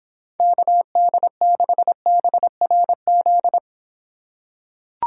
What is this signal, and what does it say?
Morse code, heard as a keyed tone, about 26 wpm, 700 Hz. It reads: KD6BRZ